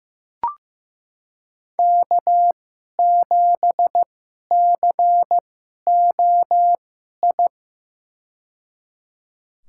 Morse code, heard as a keyed tone, about 15 wpm, spaced as for 11 wpm, 700 Hz. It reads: K7COI
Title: USA Call Signs - 5 Characters - 2x Character Spacing - 15wpm